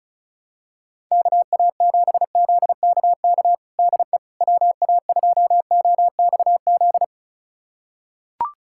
Morse code, heard as a keyed tone, about 34 wpm, 700 Hz. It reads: KA7ZKK DE WA2OXZ
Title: USA Callsign DE Callsign - 6 Characters - 35wpm